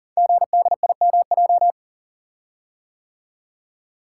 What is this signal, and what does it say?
Morse code, heard as a keyed tone, about 38 wpm, 700 Hz. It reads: GDIMJ
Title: Code Groups - 5 Letters - 40wpm